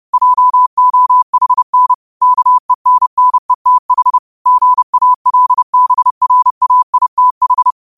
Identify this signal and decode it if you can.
Morse code, heard as a keyed tone, about 30 wpm, 1000 Hz. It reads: JOHN KENNETH GALBRAITH
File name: qotdfiles / 30wpm